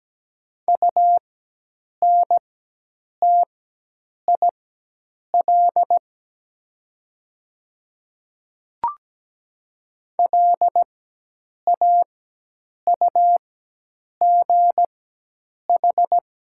Morse code, heard as a keyed tone, about 17 wpm, spaced as for 8 wpm, 700 Hz. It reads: UNTIL LAUGH
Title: 4x Character Spacing - Top 500 Words - 17wpm